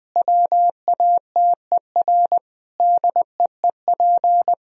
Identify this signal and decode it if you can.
Morse code, heard as a keyed tone, about 20 wpm, 700 Hz. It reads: WATER DEEP
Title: Sentences from Top 1000 Words - Review Part 4 - 20wpm